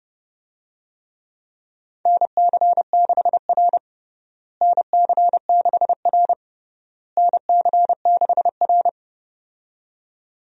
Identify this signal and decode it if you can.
Morse code, heard as a keyed tone, about 31 wpm, 700 Hz. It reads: NC6R NC6R NC6R
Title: USA Call Signs - 4 Characters - Repeated 3x - 30wpm